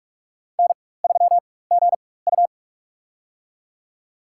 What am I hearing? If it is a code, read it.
Morse code, heard as a keyed tone, about 45 wpm, spaced as for 21 wpm, 700 Hz. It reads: N3GU
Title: USA Call Signs - 4 Characters - 4x Character Spacing - 45wpm